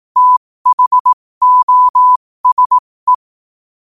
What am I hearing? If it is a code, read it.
Morse code, heard as a keyed tone, about 18 wpm, spaced as for 15 wpm, 1000 Hz. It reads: THOSE
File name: qotdfiles / 15wpm